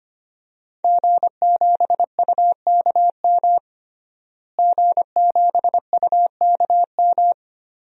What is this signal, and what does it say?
Morse code, heard as a keyed tone, about 25 wpm, 700 Hz. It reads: G7UKM G7UKM
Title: DXCC-100 Call Signs - 5 Characters - Repeated 2x - 25wpm